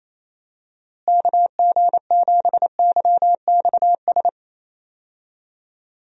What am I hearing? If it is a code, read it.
Morse code, heard as a keyed tone, about 28 wpm, 700 Hz. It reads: KG7YXS
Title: USA Call Signs - 6 Characters - Repeated 3x - 28wpm